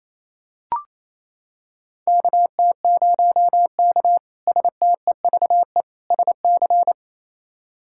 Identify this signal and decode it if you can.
Morse code, heard as a keyed tone, about 28 wpm, 700 Hz. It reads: KT0K STEVE SC
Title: K1USN SST QSO Practice - 28wpm